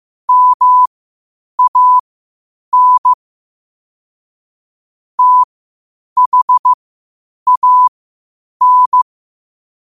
Morse code, heard as a keyed tone, about 15 wpm, spaced as for 8 wpm, 1000 Hz. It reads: MAN THAN
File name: qotdfiles / 8wpm